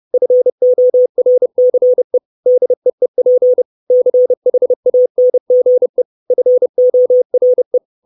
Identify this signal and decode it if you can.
Morse code, heard as a keyed tone, about 30 wpm, 500 Hz. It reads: FORCE DEEP CHANGE FORE
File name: 632 Words 30WPM